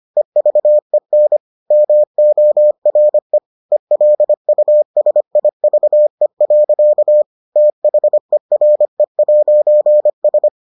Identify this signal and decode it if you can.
Morse code, heard as a keyed tone, about 25 wpm, 600 Hz. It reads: EVEN MORE ELUSIVE. THERE'S